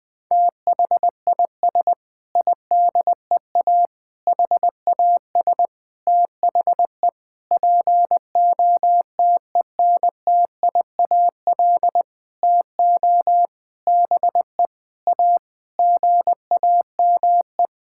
Morse code, heard as a keyed tone, about 19 wpm, 700 Hz. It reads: THIS IDEA HAS THE POTENTIAL TO BE A GAME
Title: Sentences with 6-Word Phrases - 20wpm